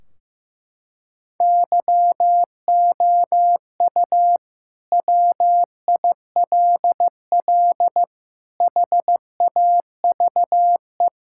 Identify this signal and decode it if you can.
Morse code, heard as a keyed tone, about 15 wpm, 700 Hz. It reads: YOU WILL HAVE
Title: Sentences from Top 1600 Words 15wpm